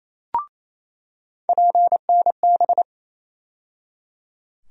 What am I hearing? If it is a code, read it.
Morse code, heard as a keyed tone, about 28 wpm, 700 Hz. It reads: PNB
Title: USA Call Sign Postfixes - 3 Letters 28wpm